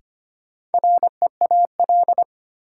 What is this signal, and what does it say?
Morse code, heard as a keyed tone, about 25 wpm, 700 Hz. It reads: REAL
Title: Sentences from Top 300 Words - New Word Drill - 25wpm